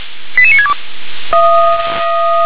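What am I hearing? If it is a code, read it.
Five-tone selective call (ZVEI), heard as quick runs of five tones, tones of about 70 ms. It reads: 88962
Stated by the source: ZVEI five-tone selective call 88765, received by SDR.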